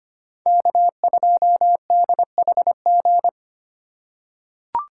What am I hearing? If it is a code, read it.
Morse code, heard as a keyed tone, about 25 wpm, 700 Hz. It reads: K2DHG
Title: USA Call Signs - 5 Characters - 2x Character Spacing - 25wpm